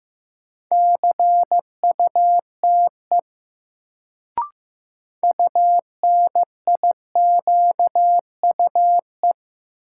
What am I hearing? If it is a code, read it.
Morse code, heard as a keyed tone, about 15 wpm, 700 Hz. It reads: CUTE UNIQUE
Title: Sentences from Top 1000 Words - Review Part 7 - New Word Drill - 15wpm